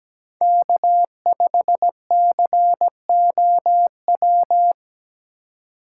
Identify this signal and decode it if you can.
Morse code, heard as a keyed tone, about 17 wpm, 700 Hz. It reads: K5COW
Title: USA Call Signs - 5 Characters - 2x Character Spacing - 17wpm